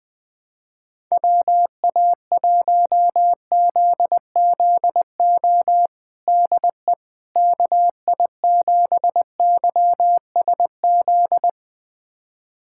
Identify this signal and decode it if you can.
Morse code, heard as a keyed tone, about 20 wpm, 700 Hz. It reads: WA1ZZO DE KI7YSZ
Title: USA Callsign DE Callsign - 6 Characters - 20wpm